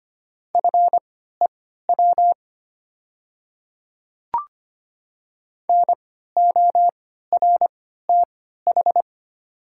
Morse code, heard as a keyed tone, about 25 wpm, spaced as for 14 wpm, 690 Hz. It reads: FEW NORTH
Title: 3x Character Spacing - Top 300 Words - 25wpm